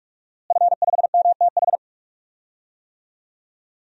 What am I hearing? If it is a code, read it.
Morse code, heard as a keyed tone, about 42 wpm, 700 Hz. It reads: F5MTH
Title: DXCC-100 Call Signs - 5 Characters - Repeated 2x - 45wpm